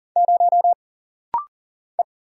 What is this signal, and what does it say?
Morse code, heard as a keyed tone, about 43 wpm, 700 Hz. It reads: 0 E
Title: Single Letter-Number - Rapid-Fire 40wpm